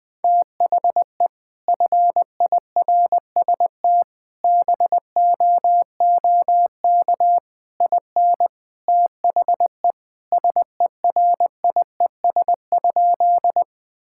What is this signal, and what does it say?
Morse code, heard as a keyed tone, about 21 wpm, 700 Hz. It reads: THE FIRST BOOK IN THE SERIES?
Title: Sentences from Top 1000 Words - Review Part 5 - 20wpm